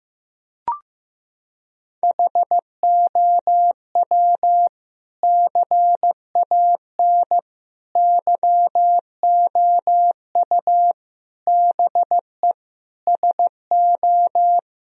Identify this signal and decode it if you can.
Morse code, heard as a keyed tone, about 15 wpm, 700 Hz. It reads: HOW CAN YOU BE SO
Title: Sentences from Top 1000 Words - Review Part 2 - 15wpm